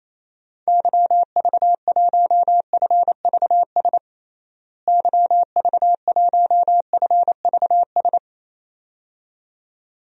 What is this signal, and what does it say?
Morse code, heard as a keyed tone, about 28 wpm, 700 Hz. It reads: YV1FVS YV1FVS
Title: DXCC-100 Call Signs - 6 Characters - Repeated 2x - 28wpm